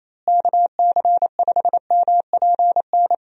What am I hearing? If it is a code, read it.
Morse code, heard as a keyed tone, about 28 wpm, 700 Hz. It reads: KC5MPN
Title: USA Call Signs - 6 Characters - Repeated 2x - 28wpm